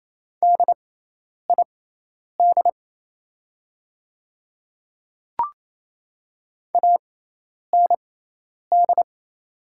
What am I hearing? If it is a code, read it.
Morse code, heard as a keyed tone, about 27 wpm, spaced as for 10 wpm, 690 Hz. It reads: DID AND
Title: 6x Character Spacing - Top 100 Words - 28wpm